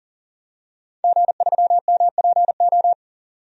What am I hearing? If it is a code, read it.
Morse code, heard as a keyed tone, about 40 wpm, 700 Hz. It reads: G3MPO
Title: DXCC-100 Call Signs - 5 Characters - 40wpm